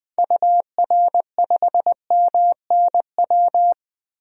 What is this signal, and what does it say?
Morse code, heard as a keyed tone, about 20 wpm, 700 Hz. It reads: UR5MNW
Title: DXCC-100 Call Signs - 6 Characters - 20wpm